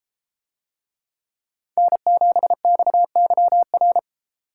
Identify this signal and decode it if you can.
Morse code, heard as a keyed tone, about 29 wpm, 700 Hz. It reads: N7XYR